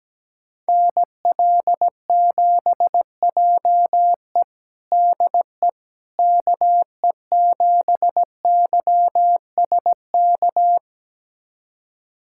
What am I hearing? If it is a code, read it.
Morse code, heard as a keyed tone, about 17 wpm, 700 Hz. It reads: NL7JE DE KE7YSK
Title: USA Callsign DE Callsign 17wpm